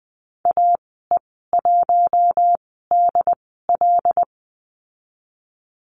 Morse code, heard as a keyed tone, about 20 wpm, spaced as for 14 wpm, 700 Hz. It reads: AE1DL
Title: 082 Lesson 14g - Number 2 - Call Signs 20wpm